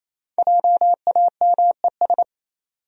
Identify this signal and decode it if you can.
Morse code, heard as a keyed tone, about 28 wpm, 700 Hz. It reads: JAMES
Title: USA Common Names - 28wpm